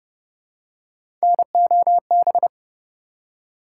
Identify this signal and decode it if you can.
Morse code, heard as a keyed tone, about 27 wpm, 700 Hz. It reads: NOB